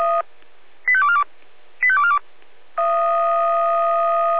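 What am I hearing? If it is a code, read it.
Five-tone selective call (ZVEI), heard as quick runs of five tones, tones of about 70 ms. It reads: 85242 85242
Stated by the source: Five-tone ZVEI selective call 85295 received on an SDR.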